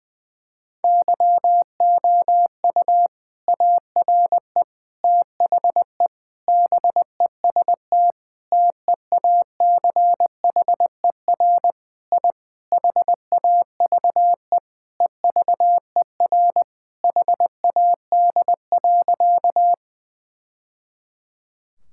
Morse code, heard as a keyed tone, about 20 wpm, 700 Hz. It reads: YOU ARE THE BEST TEACHER I HAVE EVER HAD.
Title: Sentences from Top 1000 Words - Review Part 6 - 20wpm